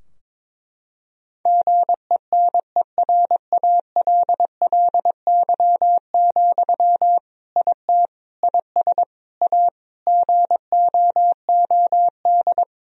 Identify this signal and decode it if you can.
Morse code, heard as a keyed tone, about 22 wpm, 700 Hz. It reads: GENERALLY, IT IS A GOOD
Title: Sentences from Top 1000 Words - Review Part 4 - 22wpm